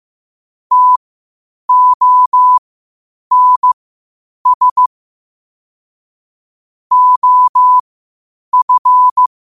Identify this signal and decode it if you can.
Morse code, heard as a keyed tone, about 15 wpm, spaced as for 8 wpm, 1000 Hz. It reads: TONS OF